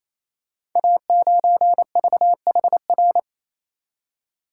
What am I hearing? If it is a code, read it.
Morse code, heard as a keyed tone, about 28 wpm, 700 Hz. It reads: A9VHR
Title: Code Groups - 5 Letter:Numbers - 28wpm